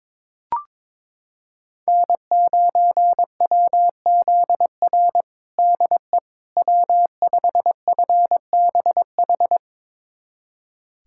Morse code, heard as a keyed tone, about 22 wpm, 700 Hz. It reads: N9WZR DE W5FBH
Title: USA Callsign DE Callsign - 5 Characters - 22wpm